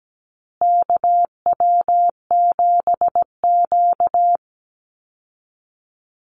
Morse code, heard as a keyed tone, about 17 wpm, 700 Hz. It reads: KW7Q
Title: Call Signs Repeated 2x - 17wpm